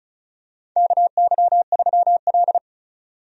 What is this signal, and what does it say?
Morse code, heard as a keyed tone, about 35 wpm, 700 Hz. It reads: KY3L
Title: USA Call Signs - 4 Characters - 35wpm